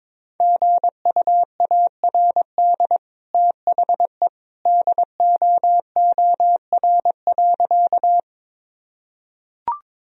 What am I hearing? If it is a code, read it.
Morse code, heard as a keyed tone, about 22 wpm, 700 Hz. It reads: GUARD THE DOOR.
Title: Sentences from Top 1000 Words - Review Part 4 - 22wpm